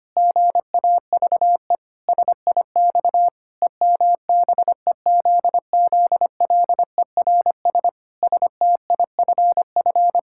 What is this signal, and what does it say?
Morse code, heard as a keyed tone, about 25 wpm, 700 Hz. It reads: GAVE SIX EMBEZZLERS STIFF
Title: Pangram Challenge v2 - 25wpm